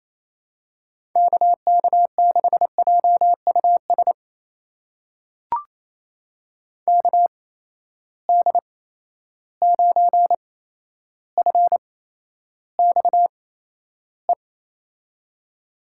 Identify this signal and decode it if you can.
Morse code, heard as a keyed tone, about 28 wpm, spaced as for 12 wpm, 700 Hz. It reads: KK6JUS KD9FXE
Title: USA Call Signs - 6 Characters - 8x Character Spacing - 28wpm